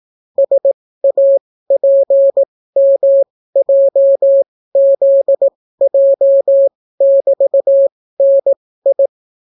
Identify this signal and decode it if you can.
Morse code, heard as a keyed tone, about 18 wpm, spaced as for 14 wpm, 550 Hz. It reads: SAPMJZJ<BT>NI